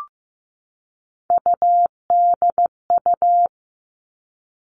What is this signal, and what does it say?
Morse code, heard as a keyed tone, about 15 wpm, 700 Hz. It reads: UDU
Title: Three Challenging Letter-Number Combinations 15wpm